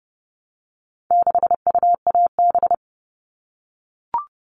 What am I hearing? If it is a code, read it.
Morse code, heard as a keyed tone, about 30 wpm, 700 Hz. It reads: 6UAB
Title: All Cut Numbers - 4-Digit 30wpm